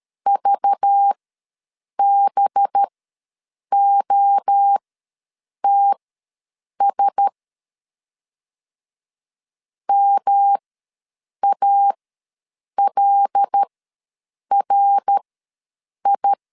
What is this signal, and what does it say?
Morse code, heard as a keyed tone, about 13 wpm, spaced as for 7 wpm, 800 Hz. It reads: VBOTS MALRI
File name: LEZIONI / 5 LBVC / 5 LBV ws 12wpm 3015